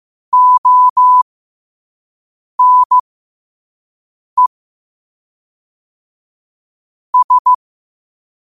Morse code, heard as a keyed tone, about 15 wpm, spaced as for 5 wpm, 1000 Hz. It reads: ONE S
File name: qotdfiles / 5wpm